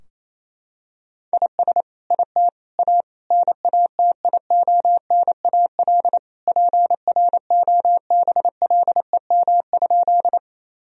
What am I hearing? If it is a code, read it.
Morse code, heard as a keyed tone, about 28 wpm, 700 Hz. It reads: IS IT A NATIONAL PROBLEM?
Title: Sentences from Top 1000 Words - Review Part 4 - 28wpm